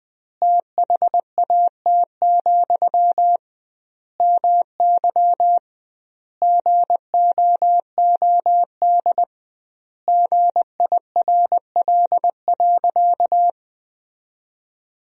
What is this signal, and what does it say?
Morse code, heard as a keyed tone, about 19 wpm, 700 Hz. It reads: THAT, MY GOOD GIRL.
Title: 2x Word-Spacing - Sentences from Top 400 Words - 20wpm